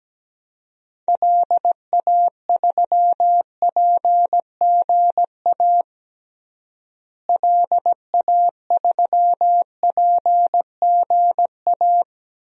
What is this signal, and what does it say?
Morse code, heard as a keyed tone, about 17 wpm, 700 Hz. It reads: LA3PGA LA3PGA
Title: DXCC-100 Call Signs - 6 Characters - Repeated 3x - 17wpm